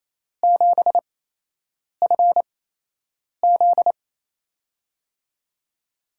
Morse code, heard as a keyed tone, about 28 wpm, spaced as for 8 wpm, 700 Hz. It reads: 7FZ
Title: USA Call Signs - 8x Character Spacing - 28wpm